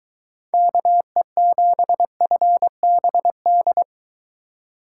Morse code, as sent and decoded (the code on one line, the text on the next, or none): -.- . --... ..-. -... -..
KE7FBD